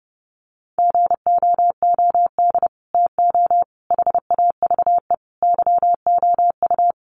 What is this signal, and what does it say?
Morse code, heard as a keyed tone, about 30 wpm, 700 Hz. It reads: GOOD TO HAVE YOU
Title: Sets of 5 Words Encore2 30wpm